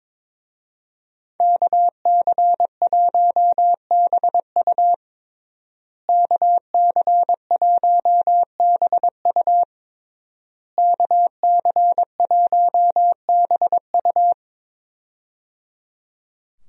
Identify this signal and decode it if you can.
Morse code, heard as a keyed tone, about 22 wpm, 700 Hz. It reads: KC1BU KC1BU KC1BU
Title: USA Call Signs - 5 Characters - Repeated 3x - 22wpm